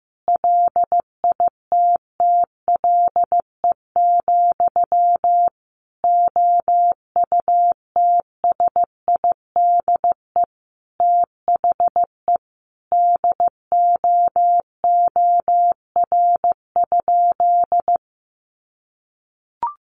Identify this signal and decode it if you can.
Morse code, heard as a keyed tone, about 15 wpm, 700 Hz. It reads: LITTLE, OUTSIDE THE DOOR?
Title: Sentences from Top 900 Words 15wpm